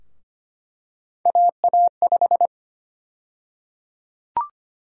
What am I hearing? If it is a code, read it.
Morse code, heard as a keyed tone, about 25 wpm, 700 Hz. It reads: AA5
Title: USA Call Sign Prefixes - Repeated 2x - 25wpm